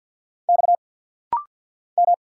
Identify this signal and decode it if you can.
Morse code, heard as a keyed tone, about 49 wpm, 700 Hz. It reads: X M
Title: Single Letter-Number Alt - Rapid-Fire 50wpm